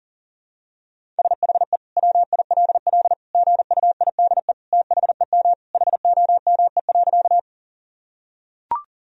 Morse code, heard as a keyed tone, about 40 wpm, 700 Hz. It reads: SHE WILL GUIDE THEM HOME.
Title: Sentences from Top 1000 Words - Review Part 3 - 40wpm